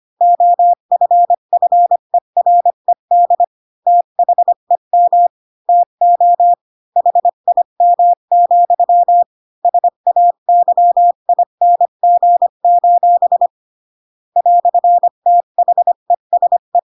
Morse code, heard as a keyed tone, about 25 wpm, 700 Hz. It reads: OFFERED THEM TO HIM, SAYING: "THESE
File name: Pinocchio / Pinocchio ch7 20 25 ik1hge